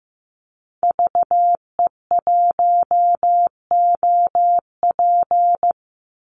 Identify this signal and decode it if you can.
Morse code, heard as a keyed tone, about 15 wpm, 700 Hz. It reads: VE1OP
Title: Call Signs Repeated 3x - 15wpm